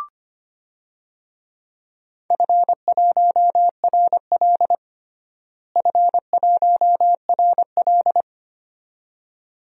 Morse code, heard as a keyed tone, about 25 wpm, 700 Hz. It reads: F1RL F1RL
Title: DXCC-100 Call Signs - 4 Characters - Repeated 2x - 25wpm